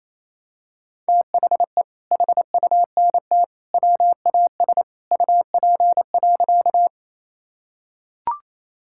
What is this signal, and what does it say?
Morse code, heard as a keyed tone, about 27 wpm, 700 Hz. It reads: THE HUNT WAS UP.
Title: Sentences from Top 1600 Words 28wpm